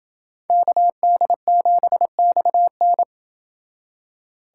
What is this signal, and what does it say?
Morse code, heard as a keyed tone, about 28 wpm, 700 Hz. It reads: KD7XN